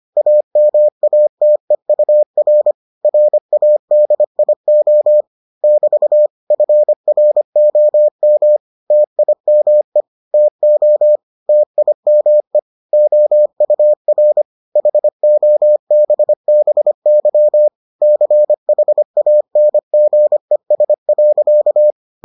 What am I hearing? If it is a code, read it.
Morse code, heard as a keyed tone, about 25 wpm, 600 Hz. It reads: AMATEUR RADIO <BT> FROM TIME TO TIME OUR HOBBY CHANGES.